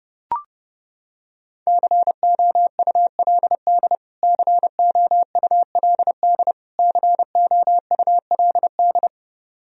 Morse code, heard as a keyed tone, about 30 wpm, 700 Hz. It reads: COULD COULD COULD
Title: Top 100 Words Repeated 30wpm